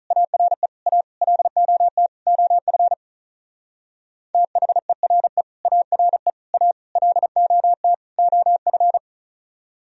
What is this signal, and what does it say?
Morse code, heard as a keyed tone, about 44 wpm, 700 Hz. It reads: ARE A LOT OF THERE ARE A LOT OF
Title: Speed-Racing - Sets of 5 Words 35wpm